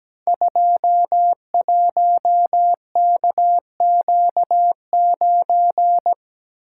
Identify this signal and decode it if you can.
Morse code, heard as a keyed tone, about 17 wpm, 700 Hz. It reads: 21KQ9
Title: Code Groups - 5 Letter:Numbers - 17wpm